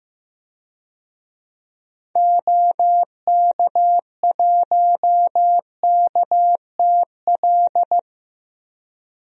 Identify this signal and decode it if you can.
Morse code, heard as a keyed tone, about 15 wpm, 700 Hz. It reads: OK1KTL
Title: DXCC-100 Call Signs - 6 Characters - Repeated 2x - 15wpm